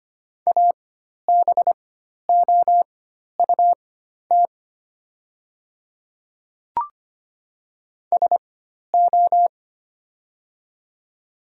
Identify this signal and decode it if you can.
Morse code, heard as a keyed tone, about 25 wpm, spaced as for 12 wpm, 700 Hz. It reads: ABOUT SO
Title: 4x Character Spacing - Top 100 Words - 25wpm